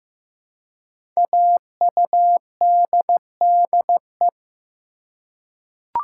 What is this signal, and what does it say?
Morse code, heard as a keyed tone, about 15 wpm, 700 Hz. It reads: AUDDE
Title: Code Groups - 5 Letters - 15wpm